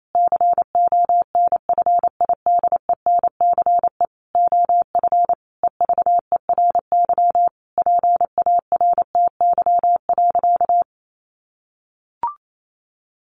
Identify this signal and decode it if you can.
Morse code, heard as a keyed tone, about 28 wpm, 700 Hz. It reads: CONFIDENCE OF EVERY PARTY.
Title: Sentences from Top 1500 Words 28wpm